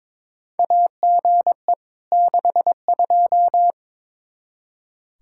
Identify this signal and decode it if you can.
Morse code, heard as a keyed tone, about 22 wpm, 700 Hz. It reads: AGE 62
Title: Practice QSO - Exchange 3 - More Info - 22wpm